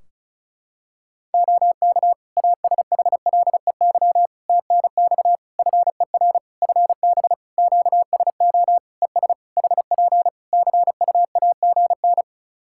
Morse code, heard as a keyed tone, about 34 wpm, 700 Hz. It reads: OK ASHLEY TNX FER FB QSO ES HP CUAGN
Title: Practice QSO - Exchange 4 - Ending - 35wpm